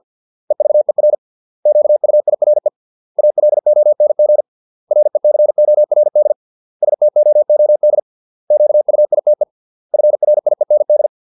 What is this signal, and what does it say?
Morse code, heard as a keyed tone, about 50 wpm, 600 Hz. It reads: EVER QUIRE ALONG RECORD STOOD QUITE FRIEND